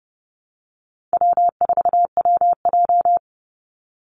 Morse code, heard as a keyed tone, about 30 wpm, 700 Hz. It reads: W4WJ
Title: Easy Call Signs Repeated 2x - 30wpm